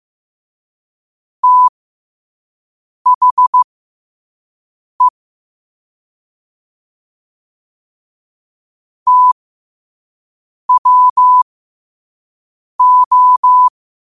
Morse code, heard as a keyed tone, about 15 wpm, spaced as for 5 wpm, 1000 Hz. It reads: THE TWO
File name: qotdfiles / 5wpm